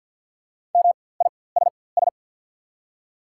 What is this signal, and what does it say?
Morse code, heard as a keyed tone, about 49 wpm, spaced as for 23 wpm, 690 Hz. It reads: MISS
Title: 4x Character Spacing - Top 500 Words - 50wpm